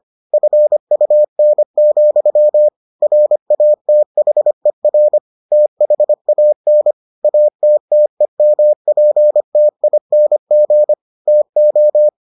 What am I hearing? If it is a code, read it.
Morse code, heard as a keyed tone, about 25 wpm, 600 Hz. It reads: FUN, RATHER THAN ATTEMPTING TO